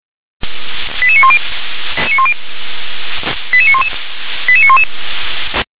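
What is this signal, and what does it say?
Five-tone selective call (ZVEI), heard as quick runs of five tones, tones of about 70 ms. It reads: 88910 88911 88911
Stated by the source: Five-tone ZVEI selective call received by SDR.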